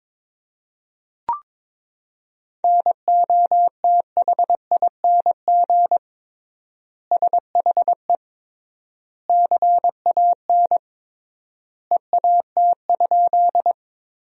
Morse code, heard as a keyed tone, about 22 wpm, 700 Hz. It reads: NOTHING SHE CAN EAT?
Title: 3x Word-Spacing - Sentences from Top 500 Words - 22wpm